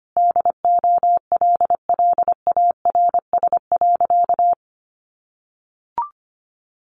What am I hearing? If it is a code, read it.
Morse code, heard as a keyed tone, about 25 wpm, 700 Hz. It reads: DOLLARS.
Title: Additional Easy Phrases 25wpm